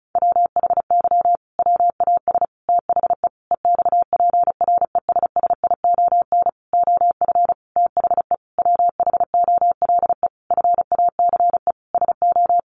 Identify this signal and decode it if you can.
Morse code, heard as a keyed tone, about 34 wpm, 700 Hz. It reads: WHY WAS THE EXPRESSION OF THE WHOLE FACE SO
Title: Sentences from Top 1400 Words 35wpm